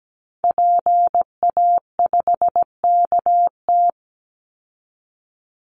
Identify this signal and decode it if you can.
Morse code, heard as a keyed tone, about 17 wpm, 700 Hz. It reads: PA5KT
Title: Call Signs Repeated 3x - 17wpm